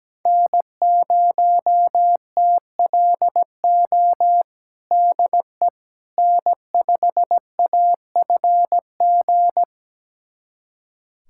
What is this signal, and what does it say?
Morse code, heard as a keyed tone, about 17 wpm, 700 Hz. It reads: N0TLO DE N5AFG
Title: USA Callsign DE Callsign - 5 Characters - 17wpm